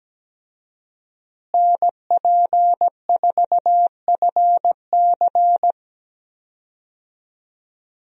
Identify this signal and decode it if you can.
Morse code, heard as a keyed tone, about 17 wpm, 700 Hz. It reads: NP4FC
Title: USA Call Signs - 5 Characters - Repeated 3x - 17wpm